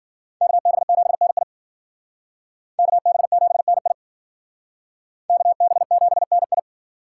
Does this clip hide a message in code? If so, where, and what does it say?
Yes: Morse code from 0.4 to 6.6 s, heard as a keyed tone, about 51 wpm, 700 Hz. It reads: KB7NI KB7NI KB7NI